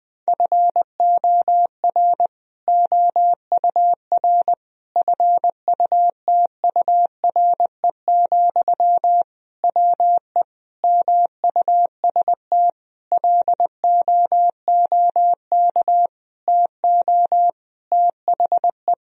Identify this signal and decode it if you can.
Morse code, heard as a keyed tone, about 20 wpm, 700 Hz. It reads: FOR OUR FUTURE, WE MUST LOOK TO THE
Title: Sentences from Top 1000 Words - Review Part 3 - 20wpm